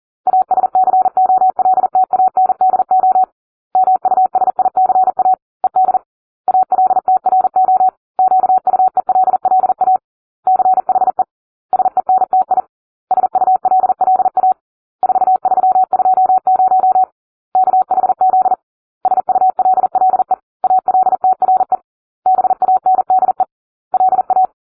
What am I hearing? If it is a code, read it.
Morse code, heard as a keyed tone, about 40 wpm, 750 Hz. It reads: ASCOLTANDO MUSICA ED ALTRO QUELLA CHE SENTI SULLA 4320 KHZ SULLE ALTRE BANDE LA